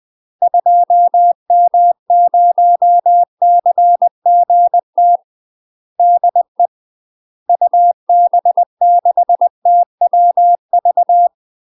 Morse code, heard as a keyed tone, about 20 wpm, 700 Hz. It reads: 2M0CGT DE UB6TWV
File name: UK MASTER QSOs TRAINING / 20WPM / UK MASTER QSOs 0009